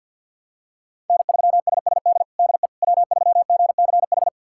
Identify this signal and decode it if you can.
Morse code, heard as a keyed tone, about 52 wpm, 700 Hz. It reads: N3SSD DE W3GCH
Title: USA Callsign DE Callsign - 5 Characters - 50wpm